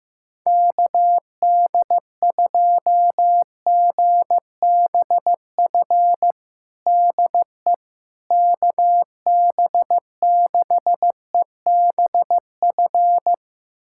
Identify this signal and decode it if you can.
Morse code, heard as a keyed tone, about 15 wpm, 700 Hz. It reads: KD2GBF DE KB6EBF
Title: USA Callsign DE Callsign 15wpm